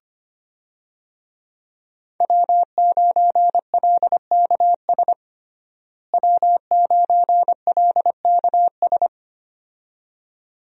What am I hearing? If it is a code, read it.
Morse code, heard as a keyed tone, about 25 wpm, 700 Hz. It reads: W9LKS W9LKS
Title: USA Call Signs - 5 Characters - Repeated 2x - 25wpm